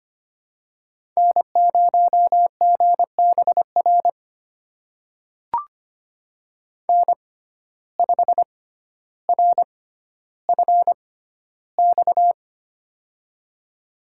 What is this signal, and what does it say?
Morse code, heard as a keyed tone, about 25 wpm, spaced as for 13 wpm, 700 Hz. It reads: N0GBR N5RFX
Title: USA Call Signs - 5 Characters - 6x Character Spacing - 25wpm